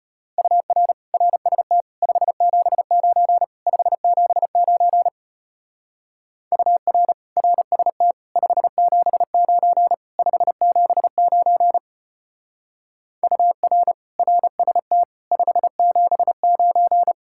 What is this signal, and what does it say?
Morse code, heard as a keyed tone, about 31 wpm, 700 Hz. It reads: UR RST 579 579 UR RST 579 579 UR RST 579
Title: Speed-Racing Common QSO Phrases Encore - 25wpm